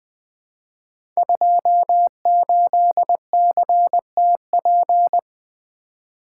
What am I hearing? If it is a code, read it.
Morse code, heard as a keyed tone, about 20 wpm, 700 Hz. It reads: 28CTP
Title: Code Groups - 5 Letter:Numbers - 20wpm